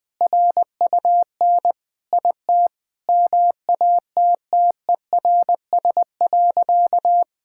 Morse code, heard as a keyed tone, about 20 wpm, 700 Hz. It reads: RUN IT MATTERS.
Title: Sentences with 4-Word Phrases - 20wpm